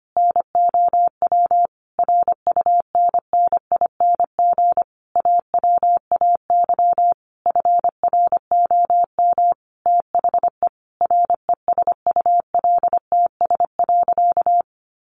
Morse code, heard as a keyed tone, about 25 wpm, 700 Hz. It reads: NOW RUNNING AWAY FROM THE RESULTS.
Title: A Sound Of Thunder 25wpm